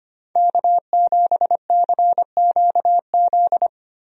Morse code, heard as a keyed tone, about 25 wpm, 700 Hz. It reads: K7CQZ
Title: USA Call Signs - 5 Characters - Repeated 3x - 25wpm